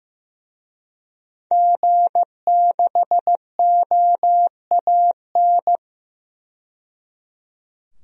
Morse code, heard as a keyed tone, about 15 wpm, 700 Hz. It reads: G6OAN
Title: DXCC-100 Call Signs - 5 Characters - 15wpm